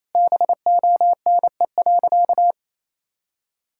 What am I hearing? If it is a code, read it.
Morse code, heard as a keyed tone, about 28 wpm, 700 Hz. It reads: BONE.
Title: Sentences from Top 1600 Words 28wpm